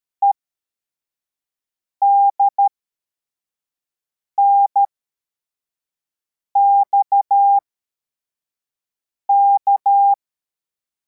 Morse code, heard as a keyed tone, about 13 wpm, spaced as for 4 wpm, 800 Hz. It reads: EDNXK
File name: formation / cw / Text25